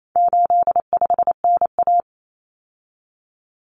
Morse code, heard as a keyed tone, about 28 wpm, 700 Hz. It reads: 85NA
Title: Common Cut Numbers - 4-Digit 28wpm